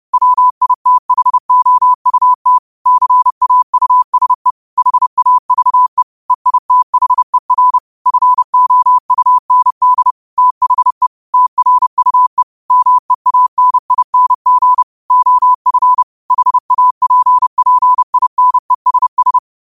Morse code, heard as a keyed tone, about 30 wpm, 1000 Hz. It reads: WITHOUT CAUSE HAVE EITHER FOUND THE TRUE MEANING OF HAPPINESS